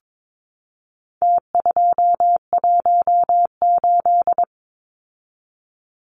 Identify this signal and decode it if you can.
Morse code, heard as a keyed tone, about 22 wpm, 700 Hz. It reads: T218
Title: Numbers - Contest Style - 4-Digit 22wpm